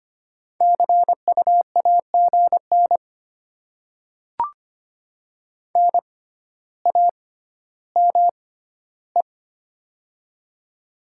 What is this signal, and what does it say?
Morse code, heard as a keyed tone, about 25 wpm, spaced as for 13 wpm, 700 Hz. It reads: CU AGN NAME
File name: Top 20 QSO Elements - 6x Character Spacing - 25wpm